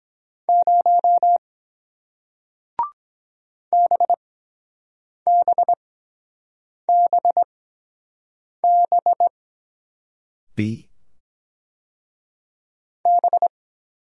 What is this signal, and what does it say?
Morse code, heard as a keyed tone, about 22 wpm, spaced as for 7 wpm, 700 Hz. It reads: 0 BBBB B